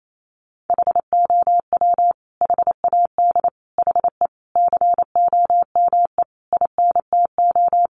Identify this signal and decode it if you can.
Morse code, heard as a keyed tone, about 28 wpm, 700 Hz. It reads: HOW HAD HE COME INTO
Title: Sentences from Top 1300 Words 28wpm